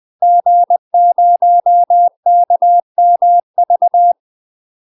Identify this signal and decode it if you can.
Morse code, heard as a keyed tone, about 20 wpm, 700 Hz. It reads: G0KMV
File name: UK MASTER QSOs TRAINING / 20WPM / UK MASTER QSOs 0037